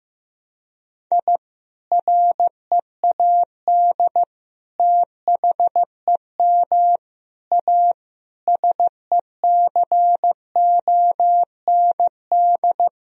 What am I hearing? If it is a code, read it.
Morse code, heard as a keyed tone, about 15 wpm, 700 Hz. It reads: I READ THEM A SECOND
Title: Sentences from Top 1000 Words - Review Part 1 - 15wpm